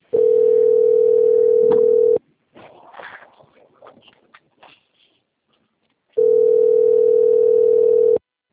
Ringback tone (North America), 440 + 480 Hz.